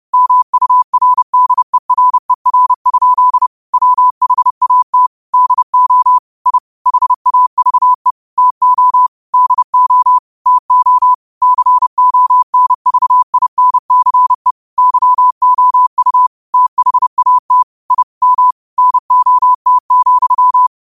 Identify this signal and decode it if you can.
Morse code, heard as a keyed tone, about 30 wpm, 1000 Hz. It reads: MURDERER? WHAT DO I HAVE TO DO TO CONVINCE YOU THAT I M NOT,
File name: qotdfiles / 30wpm